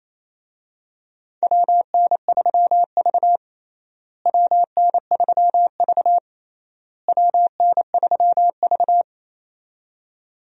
Morse code, heard as a keyed tone, about 28 wpm, 700 Hz. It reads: WN3V WN3V WN3V
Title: USA Call Signs - 4 Characters - Repeated 3x - 28wpm